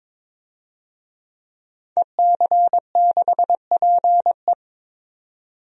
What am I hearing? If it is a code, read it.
Morse code, heard as a keyed tone, about 22 wpm, 700 Hz. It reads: EC6PE
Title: DXCC-100 Call Signs - 5 Characters - 22wpm